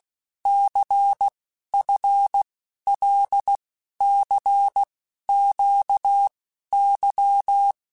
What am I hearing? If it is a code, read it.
Morse code, heard as a keyed tone, about 16 wpm, 800 Hz. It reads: C F L C Q Y
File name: Basic Letter Groups By7Groups / LFCYQ / LFCYQ@15